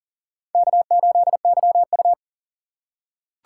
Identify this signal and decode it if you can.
Morse code, heard as a keyed tone, about 40 wpm, 700 Hz. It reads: K8YU
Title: USA Call Signs - 4 Characters - 4x Character Spacing - 40wpm